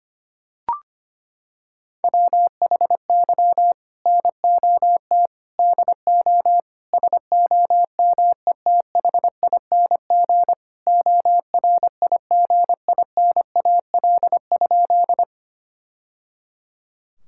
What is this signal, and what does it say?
Morse code, heard as a keyed tone, about 25 wpm, 700 Hz. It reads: WHY NOT DO SOMETHING ORIGINAL?
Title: Sentences from Top 1000 Words - Review Part 6 - 25wpm